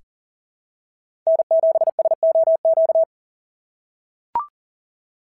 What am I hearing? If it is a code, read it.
Morse code, heard as a keyed tone, about 40 wpm, 680 Hz. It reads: N7SOQ
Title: SOTA QSO Practice 40wpm